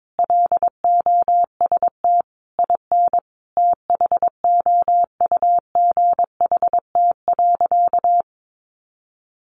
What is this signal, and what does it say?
Morse code, heard as a keyed tone, about 22 wpm, 700 Hz. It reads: LOST IN THOUGHT.
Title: Sentences from Top 1400 Words 22wpm